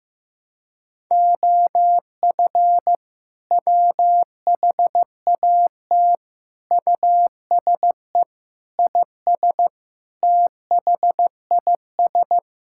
Morse code, heard as a keyed tone, about 15 wpm, 700 Hz. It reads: OF WHAT USE IS THIS
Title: Sentences from Top 1000 Words - Review Part 5 - 15wpm